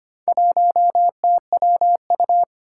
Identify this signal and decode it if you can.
Morse code, heard as a keyed tone, about 25 wpm, 700 Hz. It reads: 1TWU
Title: Four Challenging Letter-Number Combinations 25wpm